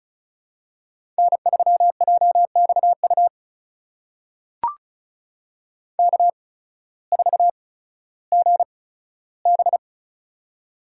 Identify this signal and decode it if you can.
Morse code, heard as a keyed tone, about 36 wpm, spaced as for 16 wpm, 700 Hz. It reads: N3JXU K4GB
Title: USA Call Signs - 8x Character Spacing - 35wpm